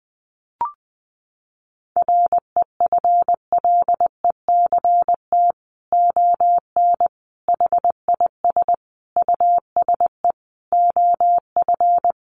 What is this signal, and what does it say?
Morse code, heard as a keyed tone, about 20 wpm, 700 Hz. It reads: REFLECT ON HIS USE OF
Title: Sentences from Top 1500 Words 20wpm